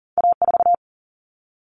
Morse code, heard as a keyed tone, about 38 wpm, 700 Hz. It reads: A4
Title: Common Cut Numbers - 2-Digit 40wpm